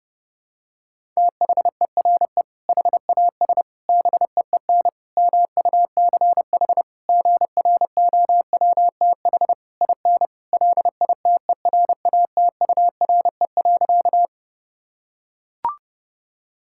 Morse code, heard as a keyed tone, about 30 wpm, 700 Hz. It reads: THERE HAS BEEN MUCH GROWTH IN LITERATURE.
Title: Sentences from Top 1000 Words - Review Part 5 - 30wpm